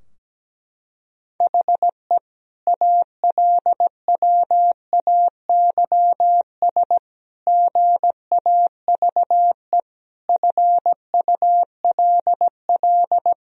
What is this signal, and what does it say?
Morse code, heard as a keyed tone, about 17 wpm, 700 Hz. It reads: HE ALWAYS GAVE FULL
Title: Sentences from Top 1000 Words - Review Part 3 - 17wpm